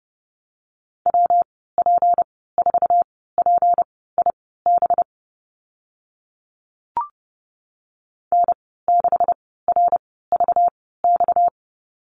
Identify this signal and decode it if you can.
Morse code, heard as a keyed tone, about 30 wpm, spaced as for 17 wpm, 700 Hz. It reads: WP4PIB N6RVX
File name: Lesson 37g - Letter x - Call Signs 30wpm